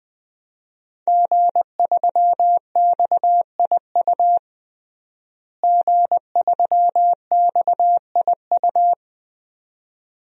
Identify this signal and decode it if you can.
Morse code, heard as a keyed tone, about 20 wpm, 700 Hz. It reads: G3XIU G3XIU
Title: DXCC-100 Call Signs - 5 Characters - Repeated 2x - 20wpm